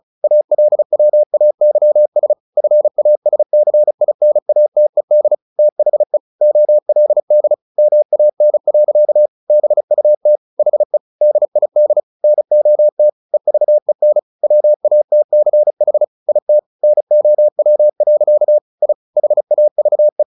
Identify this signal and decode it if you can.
Morse code, heard as a keyed tone, about 36 wpm, 600 Hz. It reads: ALWAYS FASCINATED THE OLD MAN. BUT HE DID NOT EVEN WATCH IT NOW. I HAVE
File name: the old man and the sea 2 35wpm